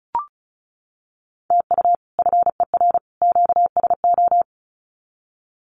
Morse code, heard as a keyed tone, about 34 wpm, 700 Hz. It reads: TU FER QSO
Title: Common QSO Phrases 35wpm